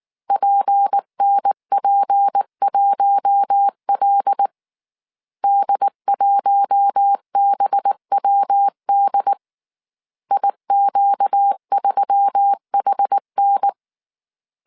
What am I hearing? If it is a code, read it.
Morse code, heard as a keyed tone, about 19 wpm, 800 Hz. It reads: PNP1L D1BWD IQ3HN
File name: CW / DET / 1 6 1500C S C 18wpm 2356